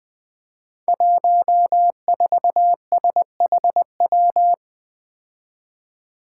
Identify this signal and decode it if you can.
Morse code, heard as a keyed tone, about 20 wpm, 700 Hz. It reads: 14SHW
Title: Five Challenging Letter-Number Combinations 20wpm